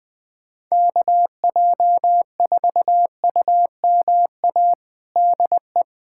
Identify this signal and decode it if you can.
Morse code, heard as a keyed tone, about 20 wpm, 700 Hz. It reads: KJ4UMA DE
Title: USA Callsign DE Callsign - 6 Characters - 20wpm